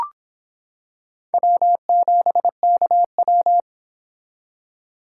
Morse code, heard as a keyed tone, about 26 wpm, 700 Hz. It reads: W7KW